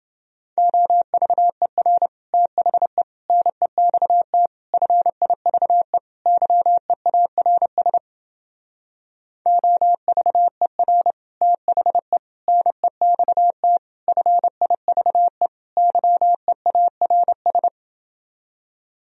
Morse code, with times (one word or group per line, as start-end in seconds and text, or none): 0.6-2.1 OVER
2.3-3.0 THE
3.3-4.5 NEXT
4.7-6.0 FIVE
6.3-8.0 YEARS
9.5-11.1 OVER
11.4-12.2 THE
12.5-13.8 NEXT
14.1-15.5 FIVE
15.8-17.7 YEARS